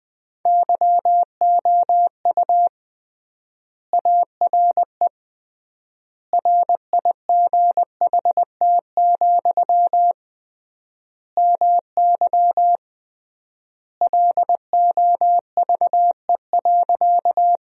Morse code, heard as a keyed tone, about 21 wpm, 700 Hz. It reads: YOU ARE RIGHT, MY LOVE.
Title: 3x Word-Spacing - Sentences from Top 400 Words - 20wpm